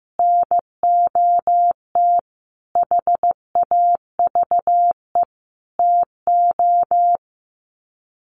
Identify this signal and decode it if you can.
Morse code, heard as a keyed tone, about 15 wpm, 700 Hz. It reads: NOT HAVE TO
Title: Sets of 5 Words 15wpm